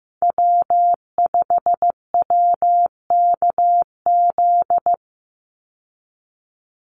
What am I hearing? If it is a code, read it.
Morse code, heard as a keyed tone, about 15 wpm, 700 Hz. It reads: W5WKZ
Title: USA Call Signs 15wpm